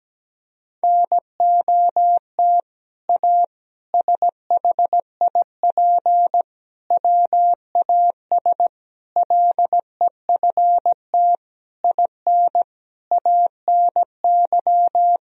Morse code, heard as a keyed tone, about 17 wpm, 700 Hz. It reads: NOT A SHIP WAS LEFT IN ANY
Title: Sentences from Top 1000 Words - Review Part 2 - 17wpm